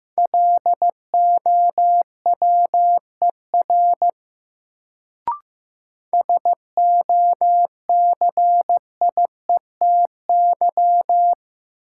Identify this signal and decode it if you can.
Morse code, heard as a keyed tone, about 15 wpm, 700 Hz. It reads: LOWER SOCIETY
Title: Sentences from Top 700 Words - New Word Drill - 15wpm